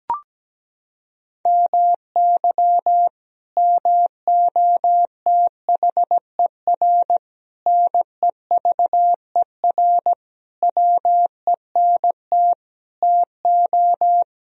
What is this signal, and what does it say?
Morse code, heard as a keyed tone, about 17 wpm, 700 Hz. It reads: MY MOTHER NEVER WENT TO
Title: Sentences from Top 1000 Words - Review Part 1 - 17wpm